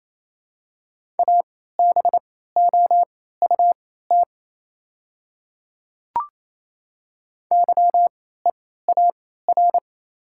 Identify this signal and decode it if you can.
Morse code, heard as a keyed tone, about 28 wpm, spaced as for 16 wpm, 700 Hz. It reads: ABOUT YEAR
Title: 3x Character Spacing - Top 200 Words - 28wpm